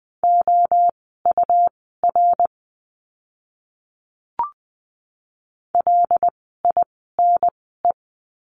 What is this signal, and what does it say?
Morse code, heard as a keyed tone, about 20 wpm, spaced as for 15 wpm, 700 Hz. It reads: OUR LINE
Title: 115 Lesson 19f - Letter w - Words 20wpm